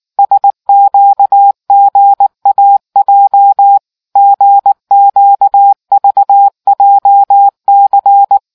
Morse code, heard as a keyed tone, about 19 wpm, 800 Hz. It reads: SQGAJ GQVJC